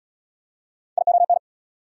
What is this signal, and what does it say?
Morse code, heard as a keyed tone, about 69 wpm, 700 Hz. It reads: ION